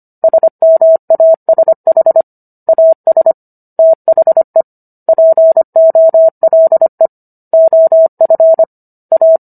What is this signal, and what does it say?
Morse code, heard as a keyed tone, about 25 wpm, 650 Hz. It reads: SMASH AS THE POLE OF A